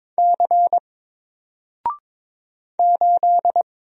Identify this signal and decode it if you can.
Morse code, heard as a keyed tone, about 22 wpm, 700 Hz. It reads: C 8
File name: Single-Letter-Number-22wpm